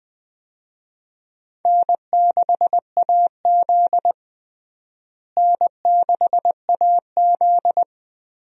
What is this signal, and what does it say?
Morse code, heard as a keyed tone, about 20 wpm, 700 Hz. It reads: N6AZ N6AZ
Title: USA Call Signs - 4 Characters - Repeated 2x - 20wpm